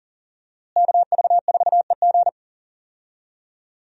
Morse code, heard as a keyed tone, about 40 wpm, 700 Hz. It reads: KV4EG
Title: USA Call Signs - 5 Characters - Repeated 3x - 40wpm